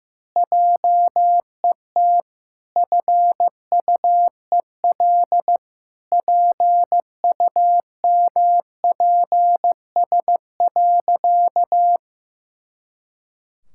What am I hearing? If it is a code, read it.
Morse code, heard as a keyed tone, about 14 wpm, 700 Hz. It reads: JET FUEL PUMPS.
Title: Pangram Challenge v2 - 15wpm